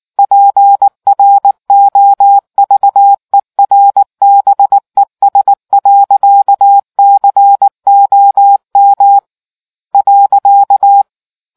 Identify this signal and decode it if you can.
Morse code, heard as a keyed tone, about 19 wpm, 800 Hz. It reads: PROVERBES.COM .